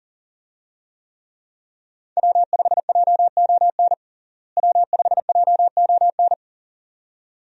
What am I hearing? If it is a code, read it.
Morse code, heard as a keyed tone, about 37 wpm, 700 Hz. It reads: W5JON W5JON
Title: USA Call Signs - 5 Characters - Repeated 2x - 40wpm